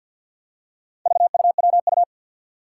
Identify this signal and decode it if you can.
Morse code, heard as a keyed tone, about 50 wpm, 700 Hz. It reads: VUWU
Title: Four Challenging Letter-Number Combinations 50wpm